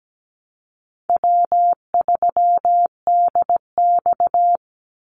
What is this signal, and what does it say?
Morse code, heard as a keyed tone, about 17 wpm, 700 Hz. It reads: W3DX
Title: NA SOTA Activator Callsigns 17wpm